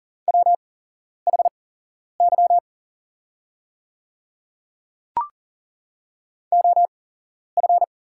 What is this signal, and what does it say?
Morse code, heard as a keyed tone, about 40 wpm, spaced as for 11 wpm, 690 Hz. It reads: WHY OF
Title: 8x Character Spacing - Top 200 Words - 40wpm